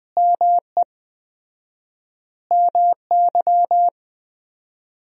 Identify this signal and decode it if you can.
Morse code, heard as a keyed tone, about 20 wpm, 700 Hz. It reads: ME MY
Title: 4x Word-Spacing - Sentences from Top 200 Words - 20wpm